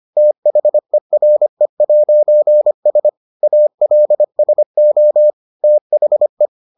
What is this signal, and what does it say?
Morse code, heard as a keyed tone, about 25 wpm, 600 Hz. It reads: THERE'S ALSO THE